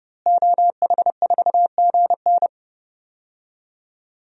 Morse code, heard as a keyed tone, about 32 wpm, 700 Hz. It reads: OH4GN
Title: DXCC-100 Call Signs - 5 Characters - Repeated 3x - 30wpm